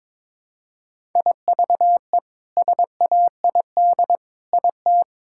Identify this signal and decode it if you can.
Morse code, heard as a keyed tone, about 22 wpm, 700 Hz. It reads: IVE SAID IT
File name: practice / 22wpm / Sets of 8 Words - 22wpm